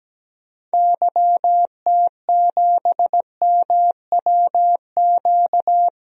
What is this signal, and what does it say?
Morse code, heard as a keyed tone, about 17 wpm, 700 Hz. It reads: YT7MWQ
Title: DXCC-100 Call Signs - 6 Characters - Repeated 3x - 17wpm